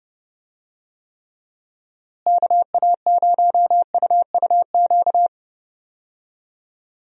Morse code, heard as a keyed tone, about 30 wpm, 700 Hz. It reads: KA0UUQ
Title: USA Call Signs - 6 Characters - Repeated 3x - 30wpm